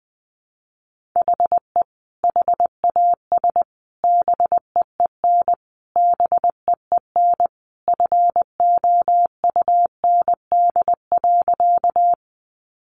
Morse code, heard as a keyed tone, about 20 wpm, 700 Hz. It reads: HE HAS BEEN BEEN FOUND.
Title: Sentences from Top 300 Words 20wpm